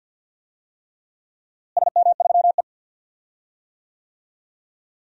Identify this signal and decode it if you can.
Morse code, heard as a keyed tone, about 50 wpm, 700 Hz. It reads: SM3E
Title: DXCC-100 Call Signs - 4 Characters - Repeated 3x - 50wpm